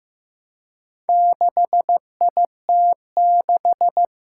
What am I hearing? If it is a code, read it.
Morse code, heard as a keyed tone, about 15 wpm, 700 Hz. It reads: 6IT6
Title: Four Challenging Letter-Number Combinations 15wpm